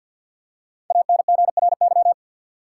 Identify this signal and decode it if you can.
Morse code, heard as a keyed tone, about 53 wpm, 700 Hz. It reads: ANGRY
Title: Five Letter Words - 50wpm